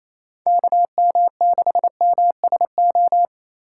Morse code, heard as a keyed tone, about 28 wpm, 700 Hz. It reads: KM6MSO
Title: USA Call Signs - 6 Characters - 8x Character Spacing - 28wpm